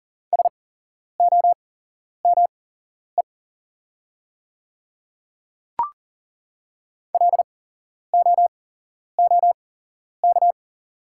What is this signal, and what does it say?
Morse code, heard as a keyed tone, about 41 wpm, spaced as for 11 wpm, 700 Hz. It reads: SOME LOOK
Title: 8x Character Spacing - Top 100 Words - 40wpm